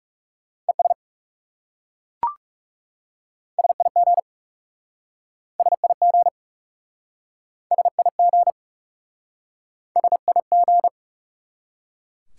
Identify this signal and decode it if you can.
Morse code, heard as a keyed tone, about 39 wpm, 700 Hz. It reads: ES SIG SIG SIG SIG